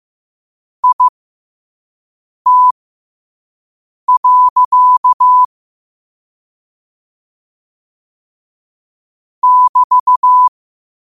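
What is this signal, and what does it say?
Morse code, heard as a keyed tone, about 15 wpm, spaced as for 5 wpm, 1000 Hz. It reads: IT. <BT>
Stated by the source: Morse code quote practice at 5 words per minute.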